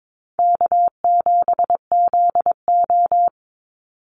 Morse code, heard as a keyed tone, about 22 wpm, 700 Hz. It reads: K7ZO
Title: NA SOTA Chaser Callsigns 22wpm